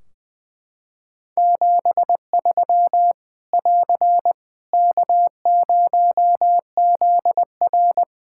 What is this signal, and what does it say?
Morse code, heard as a keyed tone, about 20 wpm, 700 Hz. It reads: 73 <AR> K0ZR
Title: K1USN SST QSO Practice 20wpm